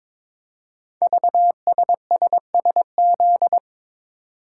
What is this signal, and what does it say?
Morse code, heard as a keyed tone, about 22 wpm, 700 Hz. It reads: VSSSZ
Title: Code Groups - 5 Letters - 22wpm